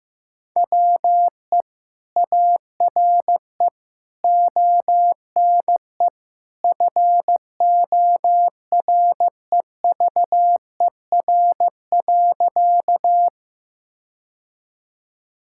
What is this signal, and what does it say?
Morse code, heard as a keyed tone, about 15 wpm, 700 Hz. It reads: WE ARE ONE FOREVER.
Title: Sentences from Top 1600 Words 15wpm